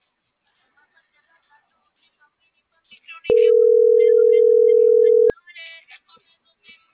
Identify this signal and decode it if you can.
Ringback tone (North America), 440 + 480 Hz.